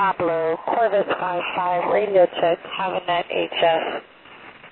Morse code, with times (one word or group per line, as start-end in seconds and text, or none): none